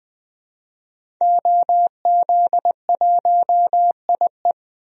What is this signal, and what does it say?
Morse code, heard as a keyed tone, about 20 wpm, 700 Hz. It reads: OZ1IE
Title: DXCC-100 Call Signs - 5 Characters - 20wpm